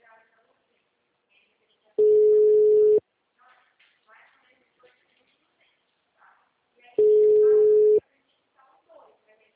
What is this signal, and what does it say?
Ringback tone (Europe, 425 Hz), with speech.
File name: out- -2043- - - .5433